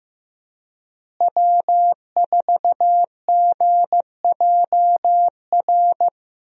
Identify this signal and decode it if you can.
Morse code, heard as a keyed tone, about 15 wpm, 700 Hz. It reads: W4GJR
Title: USA Call Signs - 5 Characters - Repeated 3x - 15wpm